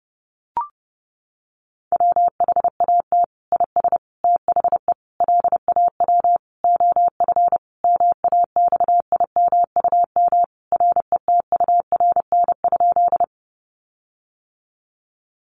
Morse code, heard as a keyed tone, about 30 wpm, 700 Hz. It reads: WHAT IS THE LAW OF MAXIMUM RETURN?
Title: Sentences from Top 1300 Words 30wpm